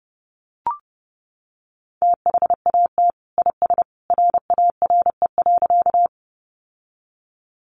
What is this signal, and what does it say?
Morse code, heard as a keyed tone, about 30 wpm, 700 Hz. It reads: THAT IS RARE.
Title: Sentences from Top 1100 Words 30wpm